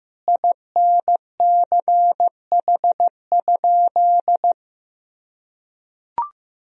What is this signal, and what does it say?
Morse code, heard as a keyed tone, about 15 wpm, 700 Hz. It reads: INCH?
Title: Sentences from Top 1000 Words - Review Part 3 - 15wpm